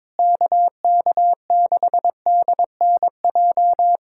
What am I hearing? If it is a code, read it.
Morse code, heard as a keyed tone, about 22 wpm, 700 Hz. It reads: KK6DNJ
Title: USA Callsign DE Callsign - 6 Characters - 22wpm